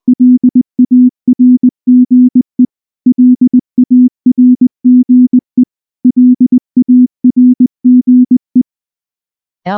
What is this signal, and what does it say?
Morse code, heard as a keyed tone, about 19 wpm, 250 Hz. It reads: LARGE LARGE LARGE
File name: morse / morse-training-20WPM-5